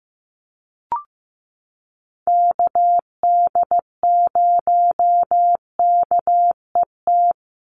Morse code, heard as a keyed tone, about 15 wpm, 700 Hz. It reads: KD0KET
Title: USA Call Signs 15wpm